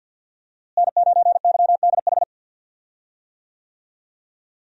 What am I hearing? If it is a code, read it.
Morse code, heard as a keyed tone, about 52 wpm, 700 Hz. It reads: N9YDH
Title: USA Call Signs - 5 Characters - Repeated 3x - 50wpm